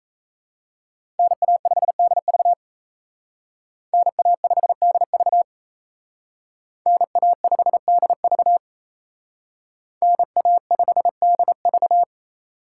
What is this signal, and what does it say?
Morse code, heard as a keyed tone, about 36 wpm, 700 Hz. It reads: NA5DV NA5DV NA5DV NA5DV